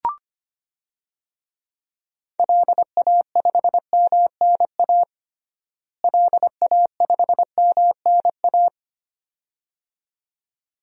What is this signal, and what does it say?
Morse code, heard as a keyed tone, about 25 wpm, 700 Hz. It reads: LA5MNA LA5MNA
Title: DXCC-100 Call Signs - 6 Characters - Repeated 2x - 25wpm